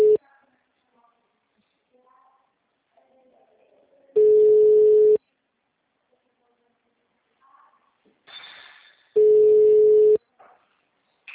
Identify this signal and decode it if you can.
Ringback tone (Europe, 425 Hz), with speech.